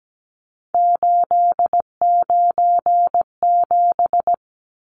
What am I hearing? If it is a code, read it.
Morse code, heard as a keyed tone, about 17 wpm, 700 Hz. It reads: 897
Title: Numbers - 3-Digit 17wpm